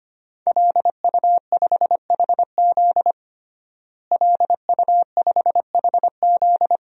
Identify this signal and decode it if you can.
Morse code, heard as a keyed tone, about 25 wpm, 700 Hz. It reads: LU5HZ LU5HZ
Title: DXCC-100 Call Signs - 5 Characters - Repeated 2x - 25wpm